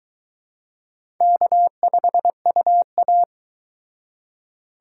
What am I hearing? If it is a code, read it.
Morse code, heard as a keyed tone, about 20 wpm, 700 Hz. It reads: K5UA
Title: Speed Racing - USA 1x2 Call Signs 17wpm